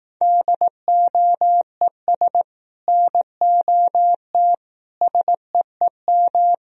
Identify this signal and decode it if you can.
Morse code, heard as a keyed tone, about 18 wpm, 700 Hz. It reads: DOES NOT SEEM
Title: Speed-Racing - Sets of 5 Words 15wpm